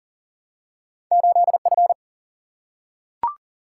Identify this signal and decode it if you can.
Morse code, heard as a keyed tone, about 43 wpm, 700 Hz. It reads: 8 F
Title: Single-Letter Single-Number 40wpm